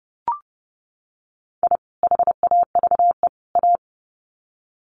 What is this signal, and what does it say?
Morse code, heard as a keyed tone, about 30 wpm, 700 Hz. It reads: I HAVE A
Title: Sets of 3 Words 30wpm